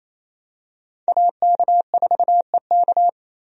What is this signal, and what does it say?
Morse code, heard as a keyed tone, about 28 wpm, 700 Hz. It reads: AK4EK
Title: USA Call Signs - 5 Characters - 28wpm